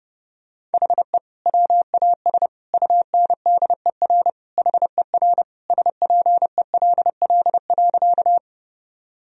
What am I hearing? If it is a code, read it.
Morse code, heard as a keyed tone, about 32 wpm, 700 Hz. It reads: HE WAS UNDER HER SPELL.
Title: Sentences from Top 1000 Words - Review Part 1 - 30wpm